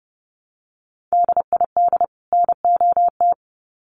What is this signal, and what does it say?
Morse code, heard as a keyed tone, about 30 wpm, 700 Hz. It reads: DID NOT
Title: Sets of 2 Words 30wpm